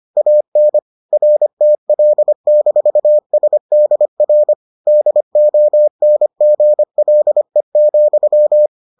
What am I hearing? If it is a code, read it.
Morse code, heard as a keyed tone, about 25 wpm, 600 Hz. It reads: AN RTL-SDR DONGLE,